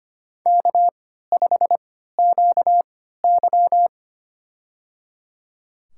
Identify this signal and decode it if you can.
Morse code, heard as a keyed tone, about 25 wpm, spaced as for 14 wpm, 700 Hz. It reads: K5QY
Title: USA Call Signs - 4 Characters - 3x Character Spacing - 25wpm